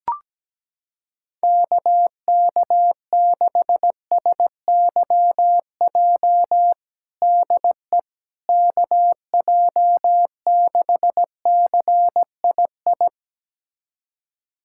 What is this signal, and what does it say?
Morse code, heard as a keyed tone, about 17 wpm, 700 Hz. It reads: KK6SYJ DE KJ6CII
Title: USA Callsign DE Callsign - 6 Characters - 17wpm